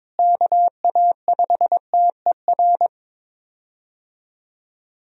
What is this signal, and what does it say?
Morse code, heard as a keyed tone, about 23 wpm, 700 Hz. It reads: KA5TER
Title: USA Call Signs - 6 Characters - Repeated 3x - 22wpm